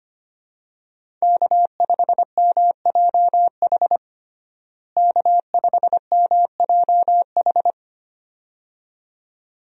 Morse code, heard as a keyed tone, about 25 wpm, 700 Hz. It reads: K5MJH K5MJH
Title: USA Call Signs - 5 Characters - Repeated 2x - 25wpm